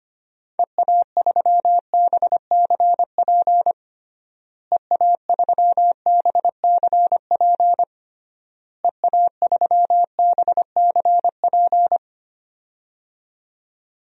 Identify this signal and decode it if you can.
Morse code, heard as a keyed tone, about 25 wpm, 700 Hz. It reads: EA3BCP EA3BCP EA3BCP
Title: DXCC-100 Call Signs - 6 Characters - Repeated 3x - 25wpm